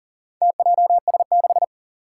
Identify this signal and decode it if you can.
Morse code, heard as a keyed tone, about 40 wpm, 700 Hz. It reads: TJS6
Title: Four Challenging Letter-Number Combinations 40wpm